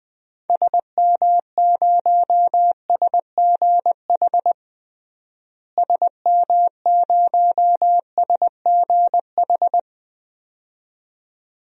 Morse code, heard as a keyed tone, about 20 wpm, 700 Hz. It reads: SM0SGH SM0SGH
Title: DXCC-100 Call Signs - 6 Characters - Repeated 2x - 20wpm